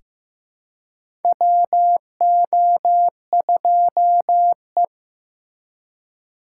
Morse code, heard as a keyed tone, about 16 wpm, 700 Hz. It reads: WO2E
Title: K1USN SST QSO Practice - 15wpm